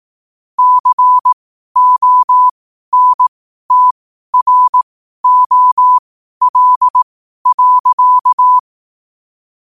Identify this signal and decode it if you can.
Morse code, heard as a keyed tone, about 18 wpm, spaced as for 12 wpm, 1000 Hz. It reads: CONTROL.
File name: qotdfiles / 12wpm